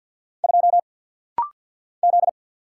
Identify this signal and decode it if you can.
Morse code, heard as a keyed tone, about 50 wpm, 690 Hz. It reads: 2 Z
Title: Single Letter-Number Alt - Rapid-Fire 50wpm